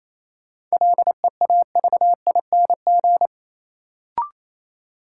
Morse code, heard as a keyed tone, about 28 wpm, 700 Hz. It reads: LEAVING
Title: Sentences from Top 1000 Words - Review Part 4 - New Word Drill - 28wpm